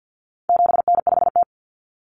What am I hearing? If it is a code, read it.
Morse code, heard as a keyed tone, about 48 wpm, 700 Hz. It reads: 7N5T
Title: Common Cut Numbers - 4-Digit 50wpm